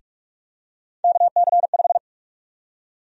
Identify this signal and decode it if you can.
Morse code, heard as a keyed tone, about 43 wpm, 700 Hz. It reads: KC5
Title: USA Call Sign Prefixes - Repeated 2x - 45wpm